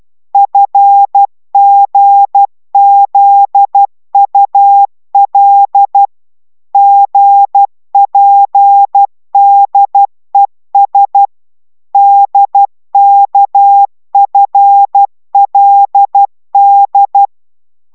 Morse code, heard as a keyed tone, about 12 wpm, 800 Hz. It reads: FGZUL GPDES DKFLD
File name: taviro tanulas / cw 69 15 10